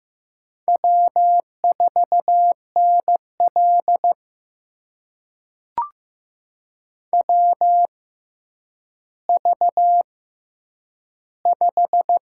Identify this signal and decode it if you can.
Morse code, heard as a keyed tone, about 15 wpm, spaced as for 9 wpm, 700 Hz. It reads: W4NL WV5
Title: USA Call Signs - 4 Characters - 6x Character Spacing - 15wpm